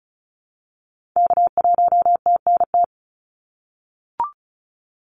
Morse code, heard as a keyed tone, about 36 wpm, 700 Hz. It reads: K1TNT
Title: NA SOTA Activator Callsigns 35wpm